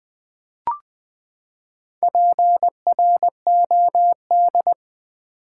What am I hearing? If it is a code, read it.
Morse code, heard as a keyed tone, about 20 wpm, 700 Hz. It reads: PROD
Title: Build A Word - 4 Letters At A Time - 12-Letter Words - 20wpm